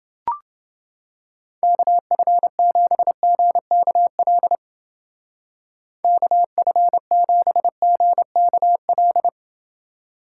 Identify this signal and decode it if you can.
Morse code, heard as a keyed tone, about 29 wpm, 700 Hz. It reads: KF7GKL KF7GKL